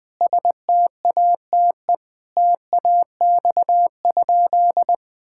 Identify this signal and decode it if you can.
Morse code, heard as a keyed tone, about 20 wpm, 700 Hz. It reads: STATE TAX?
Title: Sentences from Top 1000 Words - Review Part 5 - 20wpm